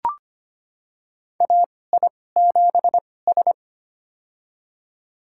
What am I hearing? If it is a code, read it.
Morse code, heard as a keyed tone, about 25 wpm, spaced as for 18 wpm, 700 Hz. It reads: AI7S
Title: USA Call Signs - 4 Characters - 2x Character Spacing - 25wpm